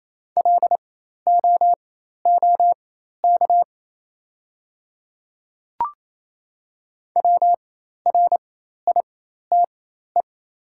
Morse code, heard as a keyed tone, about 28 wpm, spaced as for 13 wpm, 700 Hz. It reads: LOOK WRITE
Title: 4x Character Spacing - Top 100 Words - 28wpm